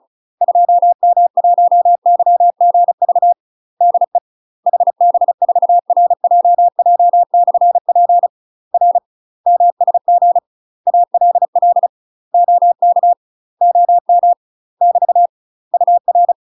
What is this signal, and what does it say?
Morse code, heard as a keyed tone, about 34 wpm, 700 Hz. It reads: 2M1YGV DE HB4RJJ/P R MSG ALL OK OM <BT> UR
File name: UK MASTER QSOs TRAINING / 35WPM / UK MASTER QSOs 0043